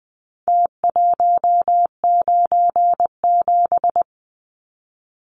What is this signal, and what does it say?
Morse code, heard as a keyed tone, about 20 wpm, 700 Hz. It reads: T197
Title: Numbers - Contest Style - 4-Digit 20wpm